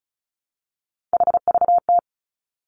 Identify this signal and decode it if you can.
Morse code, heard as a keyed tone, about 36 wpm, 700 Hz. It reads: HVT
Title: Three Challenging Letter-Number Combinations 35wpm